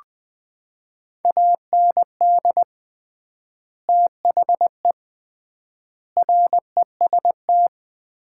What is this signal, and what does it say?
Morse code, heard as a keyed tone, about 21 wpm, 700 Hz. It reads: AND THE REST
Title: 3x Word-Spacing - Sets of 5 Words - 20wpm